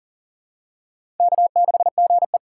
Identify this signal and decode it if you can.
Morse code, heard as a keyed tone, about 40 wpm, 700 Hz. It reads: K6GE
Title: USA Call Signs - 6x Character Spacing - 40wpm